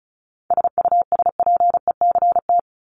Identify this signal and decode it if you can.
Morse code, heard as a keyed tone, about 34 wpm, 700 Hz. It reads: SUSPECT
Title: Top 1751-2000 Words 35wpm